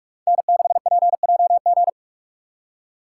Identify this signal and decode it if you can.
Morse code, heard as a keyed tone, about 41 wpm, 700 Hz. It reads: N6PJG
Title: USA Call Signs - 5 Characters - 4x Character Spacing - 45wpm